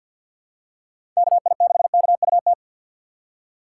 Morse code, heard as a keyed tone, about 51 wpm, 700 Hz. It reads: KI6KUT
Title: USA Call Signs - 6 Characters - 8x Character Spacing - 50wpm